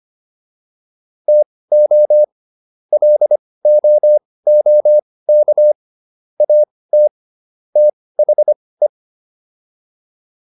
Morse code, heard as a keyed tone, about 25 wpm, spaced as for 18 wpm, 600 Hz. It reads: TO LOOK AT THE
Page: intermediate-practice-files